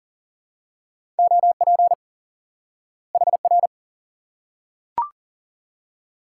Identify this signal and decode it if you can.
Morse code, heard as a keyed tone, about 40 wpm, 700 Hz. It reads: OP HR
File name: IQE Course 3 - Lesson 89 - YAGI - Two Elements - 40wpm